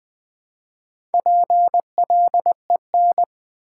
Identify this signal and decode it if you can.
Morse code, heard as a keyed tone, about 20 wpm, 700 Hz. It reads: PLEN
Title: Build A Word - 2 Letters At A Time - 6-Letter Words - 20wpm